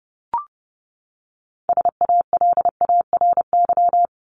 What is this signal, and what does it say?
Morse code, heard as a keyed tone, about 30 wpm, 700 Hz. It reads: SALARY
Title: Top 1751-2000 Words 30wpm